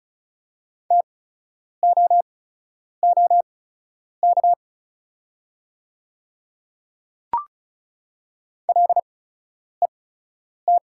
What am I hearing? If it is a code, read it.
Morse code, heard as a keyed tone, about 33 wpm, spaced as for 9 wpm, 690 Hz. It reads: TOOK LET
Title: 8x Character Spacing - Top 300 Words - 35wpm